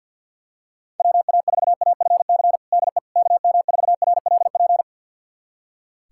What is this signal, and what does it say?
Morse code, heard as a keyed tone, about 47 wpm, 700 Hz. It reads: WA4AFC DE KM4RLP
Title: USA Callsign DE Callsign - 6 Characters - 50wpm